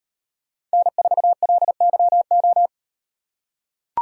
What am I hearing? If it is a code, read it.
Morse code, heard as a keyed tone, about 37 wpm, 700 Hz. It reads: N4LYO